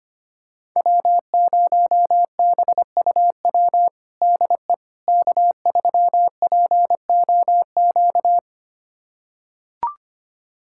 Morse code, heard as a keyed tone, about 25 wpm, 700 Hz. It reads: W0BUW DE K3POQ
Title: USA Callsign DE Callsign - 5 Characters - 25wpm